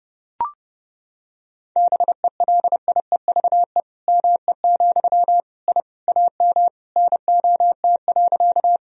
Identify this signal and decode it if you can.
Morse code, heard as a keyed tone, about 32 wpm, 700 Hz. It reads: BELIEVE ME, I AM NOT.
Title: Sentences from Top 1000 Words - Review Part 2 - 30wpm